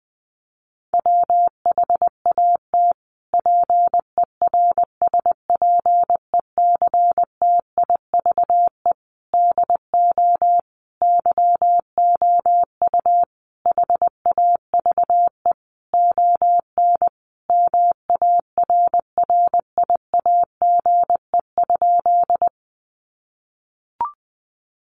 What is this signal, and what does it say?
Morse code, heard as a keyed tone, about 20 wpm, 700 Hz. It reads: WHAT PERSPECTIVE DO YOU HAVE ON MARRIAGE?
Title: Sentences from Top 1200 Words 20wpm